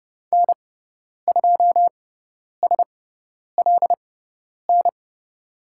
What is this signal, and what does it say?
Morse code, heard as a keyed tone, about 30 wpm, spaced as for 10 wpm, 700 Hz. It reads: N2SLN
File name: Lesson 18g - Letter m - Call Signs 30wpm